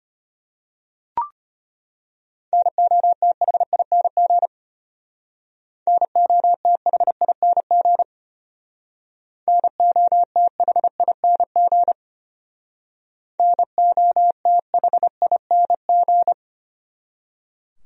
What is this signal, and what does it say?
Morse code, heard as a keyed tone, about 32 wpm, 700 Hz. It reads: NOTHING NOTHING NOTHING NOTHING